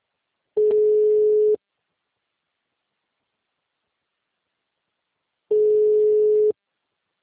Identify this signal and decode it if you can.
Ringback tone (Europe, 425 Hz).